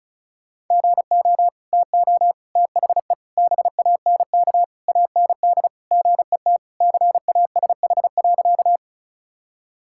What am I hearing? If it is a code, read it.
Morse code, heard as a keyed tone, about 34 wpm, 700 Hz. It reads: GO TO THE BANK AND GET CASH.
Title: Sentences from Top 1000 Words - Review Part 5 - 35wpm